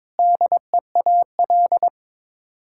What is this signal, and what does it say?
Morse code, heard as a keyed tone, about 22 wpm, 700 Hz. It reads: DEAL
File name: practice / 22wpm / Sets of 8 Words - 22wpm